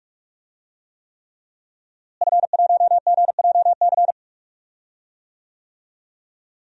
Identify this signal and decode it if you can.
Morse code, heard as a keyed tone, about 45 wpm, 700 Hz. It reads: F1GJC
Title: DXCC-100 Call Signs - 5 Characters - Repeated 3x - 45wpm